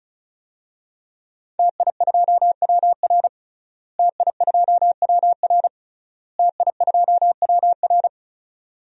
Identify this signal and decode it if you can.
Morse code, heard as a keyed tone, about 36 wpm, 700 Hz. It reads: TI2WR TI2WR TI2WR
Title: DXCC-100 Call Signs - 5 Characters - Repeated 3x - 35wpm